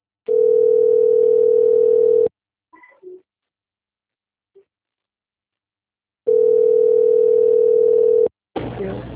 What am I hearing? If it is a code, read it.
Ringback tone (North America), 440 + 480 Hz.
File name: rg-1600-unknown- - - .66511